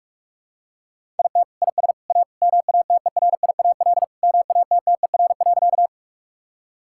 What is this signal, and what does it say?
Morse code, heard as a keyed tone, about 48 wpm, 700 Hz. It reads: IT IS A MATERIAL MATTER.
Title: Sentences from Top 1000 Words - Review Part 5 - 45wpm